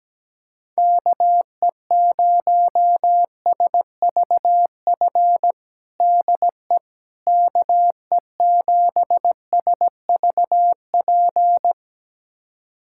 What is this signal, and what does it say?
Morse code, heard as a keyed tone, about 17 wpm, 700 Hz. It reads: KE0SVF DE KE7SVP
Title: USA Callsign DE Callsign - 6 Characters - 17wpm